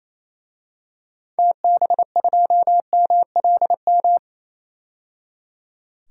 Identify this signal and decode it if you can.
Morse code, heard as a keyed tone, about 27 wpm, 700 Hz. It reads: TB2MLM
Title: DXCC-100 Call Signs - 6 Characters - 28wpm